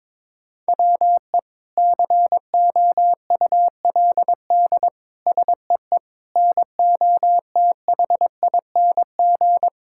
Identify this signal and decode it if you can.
Morse code, heard as a keyed tone, about 22 wpm, 700 Hz. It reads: WE COULD SEE NOTHING
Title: Sentences from Top 1000 Words - Review Part 5 - 22wpm